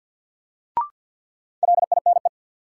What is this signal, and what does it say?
Morse code, heard as a keyed tone, about 53 wpm, 700 Hz. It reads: LINE
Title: Sets of 5 Words - New Word Drill - 50wpm